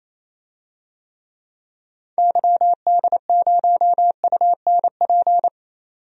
Morse code, heard as a keyed tone, about 28 wpm, 700 Hz. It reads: YD0UNP
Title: DXCC-100 Call Signs - 6 Characters - 28wpm